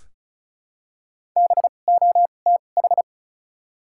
Morse code, heard as a keyed tone, about 34 wpm, spaced as for 25 wpm, 700 Hz. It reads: BOTH
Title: 2x Character Spacing - Top 300 Words - 35wpm